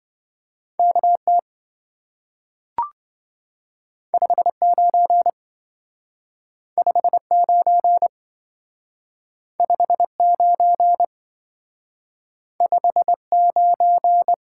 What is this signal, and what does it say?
Morse code, heard as a keyed tone, about 27 wpm, 700 Hz. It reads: KT 59 59 59 59